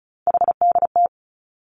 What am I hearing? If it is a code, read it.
Morse code, heard as a keyed tone, about 36 wpm, 700 Hz. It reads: HDT
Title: Three Challenging Letter-Number Combinations 35wpm